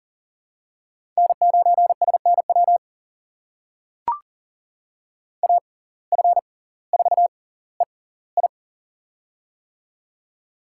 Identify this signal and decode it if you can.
Morse code, heard as a keyed tone, about 40 wpm, spaced as for 20 wpm, 700 Hz. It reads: N9SNW AF4EI
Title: USA Call Signs - 5 Characters - 6x Character Spacing - 40wpm